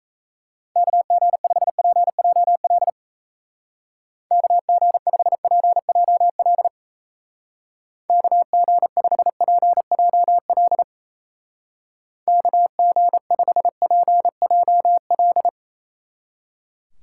Morse code, heard as a keyed tone, about 34 wpm, 700 Hz. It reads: KG5PJL KG5PJL KG5PJL KG5PJL
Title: Speed Racing - USA 2x3 Call Signs 28wpm